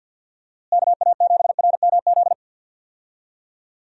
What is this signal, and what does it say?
Morse code, heard as a keyed tone, about 50 wpm, 700 Hz. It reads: KA7RMZ
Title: USA Call Signs - 6 Characters - Repeated 2x - 50wpm